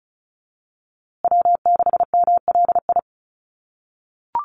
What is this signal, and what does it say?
Morse code, heard as a keyed tone, about 34 wpm, 700 Hz. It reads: W6MLI
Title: USA Call Signs Repeated 2x 35wpm